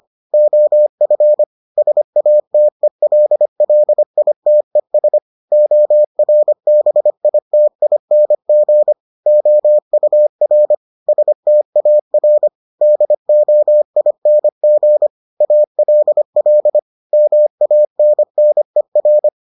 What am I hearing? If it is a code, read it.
Morse code, heard as a keyed tone, about 25 wpm, 600 Hz. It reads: OF SATELLITES ORBITING OUR STAR DOING ALL MANNER